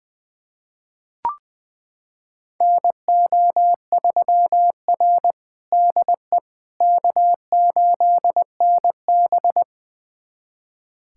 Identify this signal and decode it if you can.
Morse code, heard as a keyed tone, about 20 wpm, 700 Hz. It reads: NO3R DE K8NB
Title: USA Callsign DE Callsign - 4 Characters - 20wpm